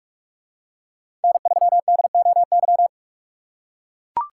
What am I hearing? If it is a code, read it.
Morse code, heard as a keyed tone, about 41 wpm, 700 Hz. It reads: N3DOY